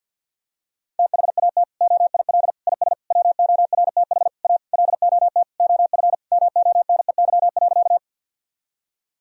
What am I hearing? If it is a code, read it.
Morse code, heard as a keyed tone, about 48 wpm, 700 Hz. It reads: THAT OIL IS WORTH A LOT OF MONEY.
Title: Sentences from Top 1000 Words - Review Part 4 - 50wpm